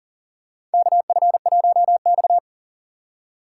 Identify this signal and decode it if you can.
Morse code, heard as a keyed tone, about 40 wpm, 700 Hz. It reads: KF1X
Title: USA Call Signs - 4 Characters - 2x Character Spacing - 40wpm